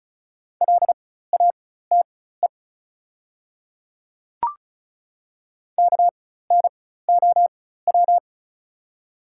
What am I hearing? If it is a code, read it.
Morse code, heard as a keyed tone, about 36 wpm, spaced as for 16 wpm, 700 Hz. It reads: LATE KNOW
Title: 4x Character Spacing - Top 300 Words - 35wpm